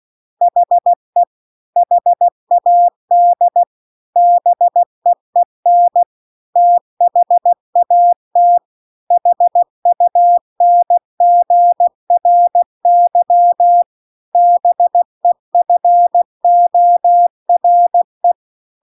Morse code, heard as a keyed tone, about 16 wpm, 700 Hz. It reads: HE HAD BEEN THAT HUNGRY BEFORE